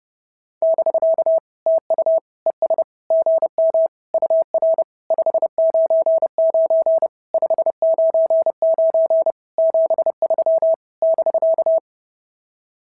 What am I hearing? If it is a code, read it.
Morse code, heard as a keyed tone, about 30 wpm, 650 Hz. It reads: <BK> TU ES GM UR 599 599 73 <BK>
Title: SOTA QSO Practice 30wpm